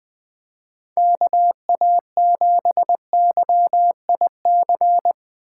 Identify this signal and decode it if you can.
Morse code, heard as a keyed tone, about 20 wpm, 700 Hz. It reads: KA7YIC